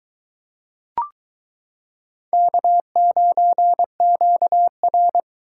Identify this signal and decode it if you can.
Morse code, heard as a keyed tone, about 22 wpm, 700 Hz. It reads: K9QR